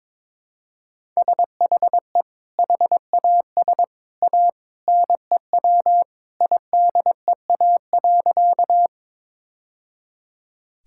Morse code, heard as a keyed tone, about 22 wpm, 700 Hz. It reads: SHE HAS A NEW IDEA.
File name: practice / 22wpm / Sentences with 2-Word Phrases - 22wpm